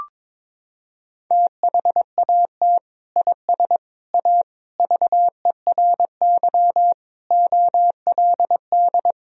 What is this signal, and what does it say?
Morse code, heard as a keyed tone, about 22 wpm, 700 Hz. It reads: THAT IS A VERY OLD
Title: Sentences from Top 1000 Words - Review Part 1 - 22wpm